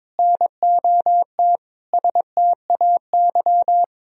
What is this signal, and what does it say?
Morse code, heard as a keyed tone, about 22 wpm, 700 Hz. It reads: NOT STAY
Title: Sentences from Top 1000 Words - Review Part 3 - 22wpm